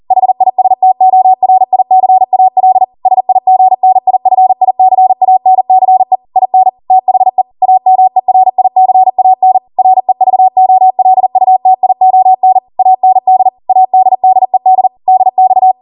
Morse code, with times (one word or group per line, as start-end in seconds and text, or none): 0.1-2.8 HISTORICAL
3.1-6.2 SIGNIFICANCE
6.4-6.7 IN
6.9-7.4 THE
7.6-9.6 AMERICAN
9.8-12.6 REVOLUTION
12.8-13.5 AND
13.7-14.9 ADDED
15.1-15.7 DX